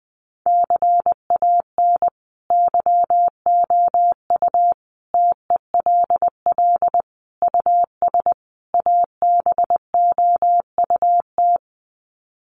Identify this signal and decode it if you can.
Morse code, heard as a keyed tone, about 20 wpm, 700 Hz. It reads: CAN YOU TELL US ABOUT
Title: Sets of 5 Words Encore2 20wpm